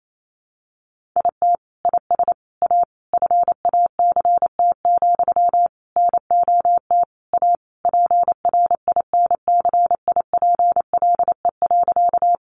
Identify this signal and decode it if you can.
Morse code, heard as a keyed tone, about 28 wpm, 700 Hz. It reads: IT IS A FACT, NOT A PRINCIPLE.
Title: Sentences from Top 1400 Words 28wpm